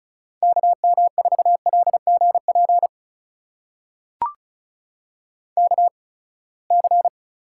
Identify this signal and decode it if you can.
Morse code, heard as a keyed tone, about 36 wpm, spaced as for 24 wpm, 700 Hz. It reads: KM4LGP KC
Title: USA Call Signs - 6 Characters - 8x Character Spacing - 35wpm